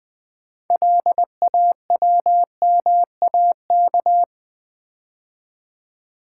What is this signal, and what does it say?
Morse code, heard as a keyed tone, about 20 wpm, 700 Hz. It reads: LAWMAK
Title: Build A Word - 3 Letters At A Time - 8-Letter Words - 20wpm